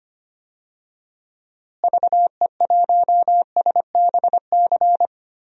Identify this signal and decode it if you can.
Morse code, heard as a keyed tone, about 25 wpm, 700 Hz. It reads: VE1SBC
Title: DXCC-100 Call Signs - 6 Characters - Repeated 2x - 25wpm